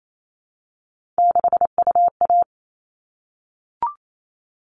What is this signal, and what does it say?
Morse code, heard as a keyed tone, about 28 wpm, 700 Hz. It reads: 6UA
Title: All Cut Numbers - 3-Digit 28wpm